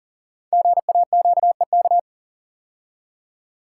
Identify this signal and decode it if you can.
Morse code, heard as a keyed tone, about 39 wpm, 700 Hz. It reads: GAQEK
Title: Code Groups - 5 Letters - 40wpm